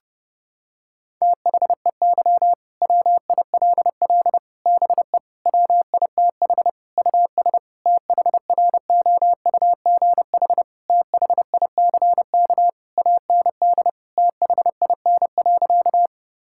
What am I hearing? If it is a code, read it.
Morse code, heard as a keyed tone, about 30 wpm, 700 Hz. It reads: THEY WILL BE WITH US THROUGH THICK AND THIN.
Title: Sentences from Top 1600 Words 30wpm